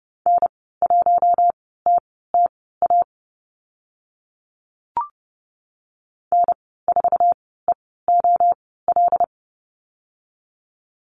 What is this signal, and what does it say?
Morse code, heard as a keyed tone, about 30 wpm, spaced as for 17 wpm, 700 Hz. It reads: N1TTA N4EOL
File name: Lesson 15g - Number 5 - Call Signs 30wpm